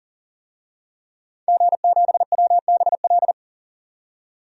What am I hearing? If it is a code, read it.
Morse code, heard as a keyed tone, about 40 wpm, 700 Hz. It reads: G7WBL
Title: DXCC-100 Call Signs - 5 Characters - 40wpm